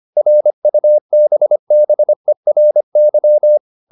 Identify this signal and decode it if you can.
Morse code, heard as a keyed tone, about 25 wpm, 600 Hz. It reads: RUBBERY